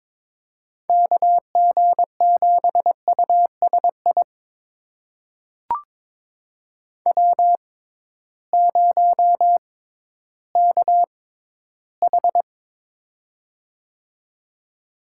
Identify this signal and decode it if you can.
Morse code, heard as a keyed tone, about 21 wpm, spaced as for 13 wpm, 700 Hz. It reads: KG7USI W0KH
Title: USA Call Signs - 6x Character Spacing - 22wpm